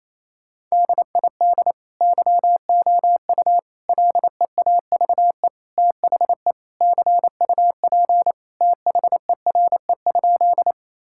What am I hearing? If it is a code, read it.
Morse code, heard as a keyed tone, about 28 wpm, 700 Hz. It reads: DID YOU LEAVE THE CUP THERE?
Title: Sentences from Top 1600 Words 28wpm